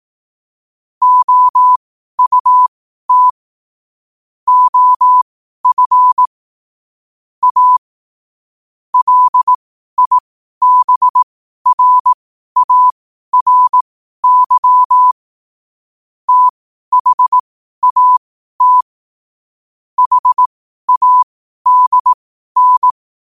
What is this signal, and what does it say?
Morse code, heard as a keyed tone, about 18 wpm, spaced as for 12 wpm, 1000 Hz. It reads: OUT OF A LIBRARY THAT HADN
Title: 12 WPM morse code quotes for Thu, 29 Feb 2024